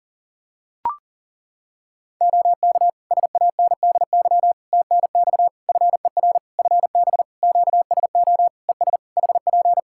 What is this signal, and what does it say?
Morse code, heard as a keyed tone, about 40 wpm, 700 Hz. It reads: OK SANDY TNX FER FB QSO ES HP
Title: K1USN SST QSO Practice 40wpm